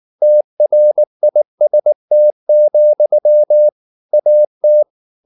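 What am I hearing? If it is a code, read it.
Morse code, heard as a keyed tone, about 19 wpm, 600 Hz. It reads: TRIST, AT
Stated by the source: Slow (QRS) Morse practice at 18 wpm, CW med Gnister.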